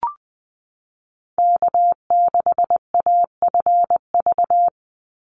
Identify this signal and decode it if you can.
Morse code, heard as a keyed tone, about 20 wpm, 700 Hz. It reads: K6AFV
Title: USA Call Signs 20wpm